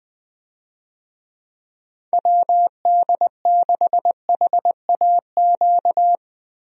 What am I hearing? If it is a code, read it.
Morse code, heard as a keyed tone, about 20 wpm, 700 Hz. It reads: WD6HAQ
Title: USA Call Signs - 6 Characters - Repeated 2x - 20wpm